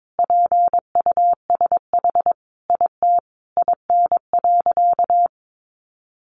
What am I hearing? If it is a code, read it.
Morse code, heard as a keyed tone, about 22 wpm, 700 Hz. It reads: PUSH IT IN.
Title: Sentences from Top 1300 Words 22wpm